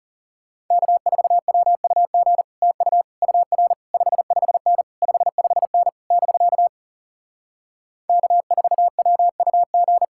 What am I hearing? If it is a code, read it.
Morse code, heard as a keyed tone, about 40 wpm, 700 Hz. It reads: K4WUG TU UR 55N 55N <BK> K4WUG